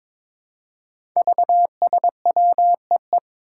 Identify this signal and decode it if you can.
Morse code, heard as a keyed tone, about 22 wpm, 700 Hz. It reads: VSWEE
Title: Five Challenging Letter-Number Combinations 22wpm